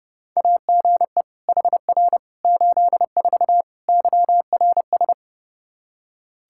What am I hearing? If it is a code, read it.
Morse code, heard as a keyed tone, about 30 wpm, 700 Hz. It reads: AGE HR 84 YRS
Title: Practice QSO - Exchange 3 - More Info - 30wpm